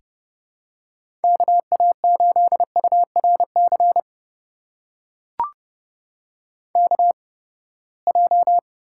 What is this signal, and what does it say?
Morse code, heard as a keyed tone, about 30 wpm, spaced as for 21 wpm, 700 Hz. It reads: KA8URC KJ
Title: USA Call Signs - 6 Characters - 8x Character Spacing - 30wpm